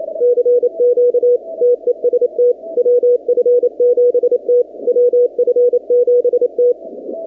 Morse code, heard as a keyed tone, about 27 wpm, 490 Hz. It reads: CQ TEST WF7T WF7T
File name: 2019-cw-wpx-kt5j-k5pi